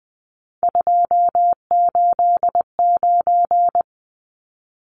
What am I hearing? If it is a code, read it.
Morse code, heard as a keyed tone, about 20 wpm, 700 Hz. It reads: 289
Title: Numbers - 3-Digit 20wpm